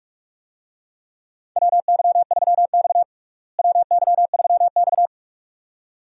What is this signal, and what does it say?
Morse code, heard as a keyed tone, about 46 wpm, 700 Hz. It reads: WY3X WY3X
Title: USA Call Signs - 4 Characters - Repeated 2x - 45wpm